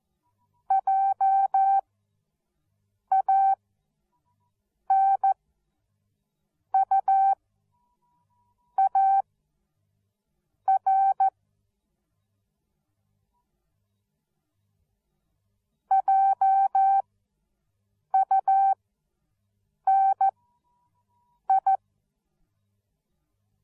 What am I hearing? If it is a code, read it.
Morse code, heard as a keyed tone, about 14 wpm, spaced as for 5 wpm, 780 Hz. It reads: JANUAR JUNI